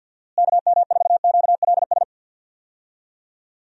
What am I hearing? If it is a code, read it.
Morse code, heard as a keyed tone, about 51 wpm, 700 Hz. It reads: KM4QLS
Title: USA Call Signs - 6 Characters - Repeated 2x - 50wpm